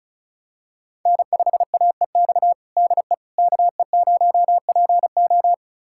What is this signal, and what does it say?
Morse code, heard as a keyed tone, about 36 wpm, 700 Hz. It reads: N5AEX DE KE0PO
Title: USA Callsign DE Callsign - 5 Characters - 35wpm